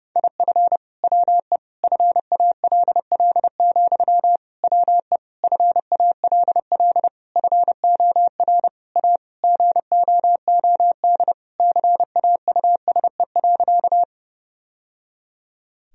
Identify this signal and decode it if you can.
Morse code, heard as a keyed tone, about 32 wpm, 700 Hz. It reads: IF WE FALL, WE FALL FOR A GOOD CAUSE.
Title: Sentences from Top 1000 Words - Review Part 2 - 30wpm